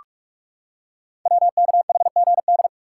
Morse code, heard as a keyed tone, about 44 wpm, 700 Hz. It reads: WKHGD
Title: Code Groups - 5 Letters - 45wpm